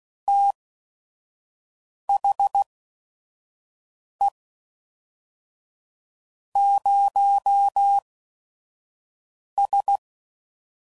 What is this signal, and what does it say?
Morse code, heard as a keyed tone, about 16 wpm, spaced as for 5 wpm, 800 Hz. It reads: THE 0S
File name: CW Code Groups @5WPM / Technical Subjects / Gates Intro